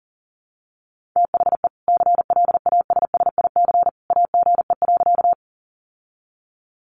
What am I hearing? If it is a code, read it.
Morse code, heard as a keyed tone, about 40 wpm, 700 Hz. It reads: THE CLASSIC AGE.
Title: Sentences from Top 1000 Words 40wpm